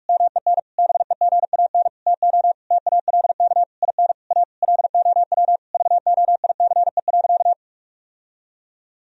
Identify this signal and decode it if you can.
Morse code, heard as a keyed tone, about 41 wpm, 700 Hz. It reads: MEN BEGAN TO TALK IN A LOW VOICE.
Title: Sentences from Top 1000 Words - Review Part 1 - 45wpm